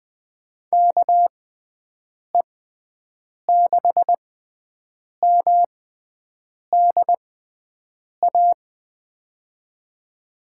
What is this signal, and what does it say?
Morse code, heard as a keyed tone, about 20 wpm, spaced as for 7 wpm, 700 Hz. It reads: KE6MDA
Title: USA Call Signs - 6 Characters - 6x Character Spacing - 20wpm